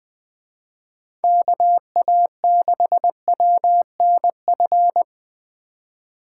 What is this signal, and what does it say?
Morse code, heard as a keyed tone, about 24 wpm, 700 Hz. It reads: KA6WNF
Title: Speed Racing - USA 2x3 Call Signs 20wpm